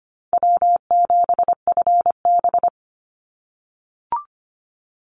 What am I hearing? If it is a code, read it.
Morse code, heard as a keyed tone, about 25 wpm, 700 Hz. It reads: W7FB
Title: Call Signs - 25wpm